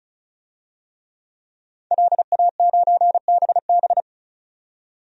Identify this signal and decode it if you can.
Morse code, heard as a keyed tone, about 36 wpm, 700 Hz. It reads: LA9BB
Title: DXCC-100 Call Signs - 5 Characters - 35wpm